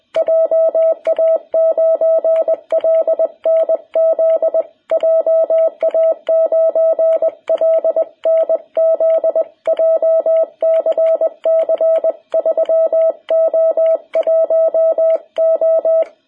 Morse code, heard as a keyed tone, about 20 wpm, 660 Hz. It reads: JA9LNZ JA9LNZ JCC 3O1O
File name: jh8cbh / 2021cw